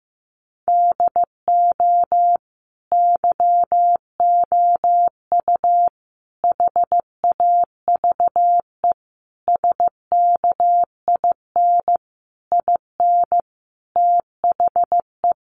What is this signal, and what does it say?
Morse code, heard as a keyed tone, about 15 wpm, 700 Hz. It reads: DO YOU HAVE SKIN IN THE
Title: Sentences from Top 1300 Words 15wpm